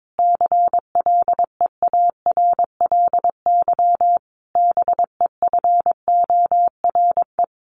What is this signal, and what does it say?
Morse code, heard as a keyed tone, about 22 wpm, 700 Hz. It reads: CLEARLY BEFORE
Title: Sentences from Top 1300 Words 22wpm